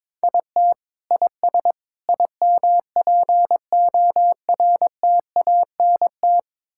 Morse code, heard as a keyed tone, about 21 wpm, 700 Hz. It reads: IT IS IMPORTANT
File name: Sentences with 7-Word Phrases - 22wpm